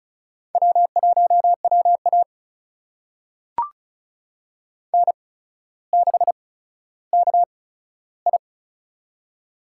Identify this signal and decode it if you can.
Morse code, heard as a keyed tone, about 34 wpm, spaced as for 15 wpm, 700 Hz. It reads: W1WA N6KI
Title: USA Call Signs - 4 Characters - 8x Character Spacing - 35wpm